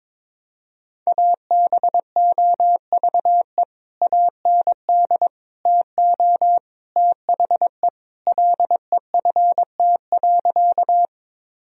Morse code, heard as a keyed tone, about 22 wpm, 700 Hz. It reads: ABOVE AND TO THE LEFT.
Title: Sentences from Top 1000 Words - Review Part 2 - 22wpm